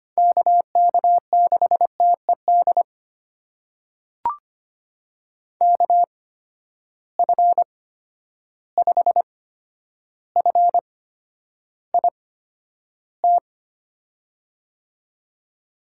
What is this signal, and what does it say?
Morse code, heard as a keyed tone, about 25 wpm, spaced as for 11 wpm, 700 Hz. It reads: KK6TED KF5FIT
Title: USA Call Signs - 6 Characters - 8x Character Spacing - 25wpm